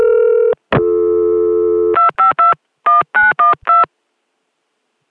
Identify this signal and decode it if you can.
DTMF, heard as telephone touch tones, tones of about 150 ms. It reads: 2521912